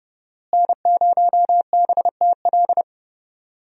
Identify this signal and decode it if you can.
Morse code, heard as a keyed tone, about 31 wpm, 700 Hz. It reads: N0BTL
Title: USA Call Signs - 5 Characters - Repeated 3x - 30wpm